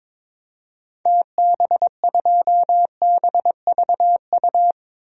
Morse code, heard as a keyed tone, about 22 wpm, 700 Hz. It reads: TB2BVU
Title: DXCC-100 Call Signs - 6 Characters - 22wpm